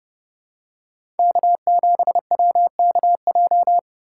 Morse code, heard as a keyed tone, about 30 wpm, 700 Hz. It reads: K7WKJ
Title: USA Call Signs - 5 Characters - 2x Character Spacing - 30wpm